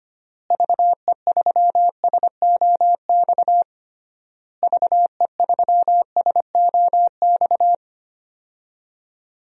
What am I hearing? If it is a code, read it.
Morse code, heard as a keyed tone, about 25 wpm, 700 Hz. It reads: VE3SOX VE3SOX
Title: DXCC-100 Call Signs - 6 Characters - Repeated 3x - 25wpm